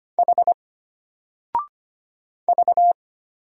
Morse code, heard as a keyed tone, about 25 wpm, 700 Hz. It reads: H V
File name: Single-Letter-Number-25wpm